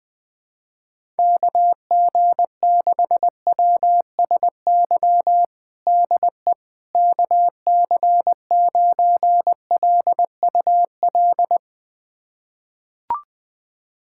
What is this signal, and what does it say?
Morse code, heard as a keyed tone, about 20 wpm, 700 Hz. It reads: KG6WSY DE KC9LUL
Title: USA Callsign DE Callsign - 6 Characters - 20wpm